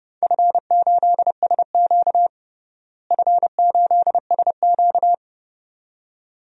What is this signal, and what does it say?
Morse code, heard as a keyed tone, about 30 wpm, 700 Hz. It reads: F8SQ F8SQ
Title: DXCC-100 Call Signs - 4 Characters - Repeated 2x - 30wpm